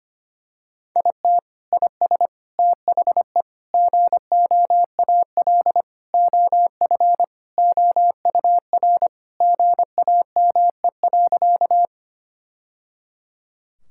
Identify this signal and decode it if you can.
Morse code, heard as a keyed tone, about 25 wpm, 700 Hz. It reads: IT IS THE GOAL OF OUR GAME.
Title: Sentences from Top 1000 Words - Review Part 5 - 25wpm